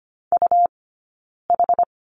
Morse code, heard as a keyed tone, about 25 wpm, 700 Hz. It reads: U H
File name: Lesson 17d - Letter u - Two Characters 25wpm